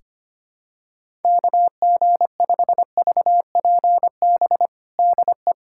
Morse code, heard as a keyed tone, about 25 wpm, 700 Hz. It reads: KG5VPB DE
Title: USA Callsign DE Callsign 25wpm